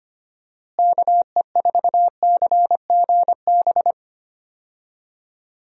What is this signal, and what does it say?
Morse code, heard as a keyed tone, about 25 wpm, 700 Hz. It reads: KE4CGB
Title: USA Call Signs - 6 Characters - Repeated 3x - 25wpm